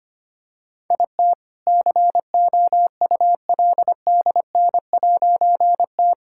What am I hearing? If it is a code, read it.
Morse code, heard as a keyed tone, about 25 wpm, 700 Hz. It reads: IT COULDN'T
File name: practice / 25wpm / All-Age Appropriate Jokes - 25wpm